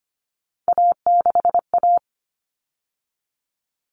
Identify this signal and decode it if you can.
Morse code, heard as a keyed tone, about 25 wpm, 700 Hz. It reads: A6A
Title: Common Cut Numbers - 3-Digit 25wpm